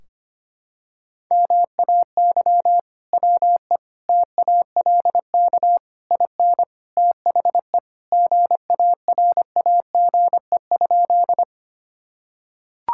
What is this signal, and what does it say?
Morse code, heard as a keyed tone, about 25 wpm, 700 Hz. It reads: MAY WE TALK IN THE GARAGE?
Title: Sentences from Top 1600 Words 25wpm